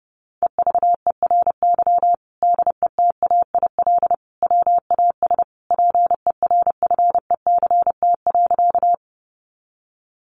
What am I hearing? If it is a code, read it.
Morse code, heard as a keyed tone, about 30 wpm, 700 Hz. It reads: EVERY DETAIL WAS PERFECT.
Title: Sentences from Top 1500 Words 30wpm